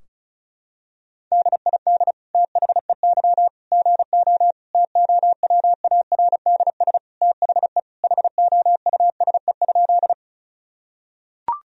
Morse code, heard as a keyed tone, about 36 wpm, 700 Hz. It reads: DID THEY GO TOWARDS THE HOUSE?
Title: Sentences from Top 1000 Words - Review Part 2 - 35wpm